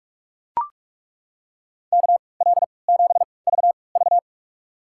Morse code, heard as a keyed tone, about 41 wpm, spaced as for 25 wpm, 700 Hz. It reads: KL7VV
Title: USA Call Signs - 5 Characters - 3x Character Spacing - 45wpm